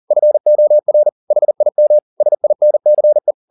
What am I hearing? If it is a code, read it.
Morse code, heard as a keyed tone, about 40 wpm, 600 Hz. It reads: FOR HIM SINCE